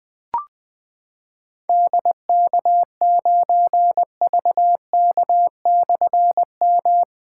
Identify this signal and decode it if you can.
Morse code, heard as a keyed tone, about 20 wpm, 700 Hz. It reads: DK9VK/M
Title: Difficult Call Signs 20wpm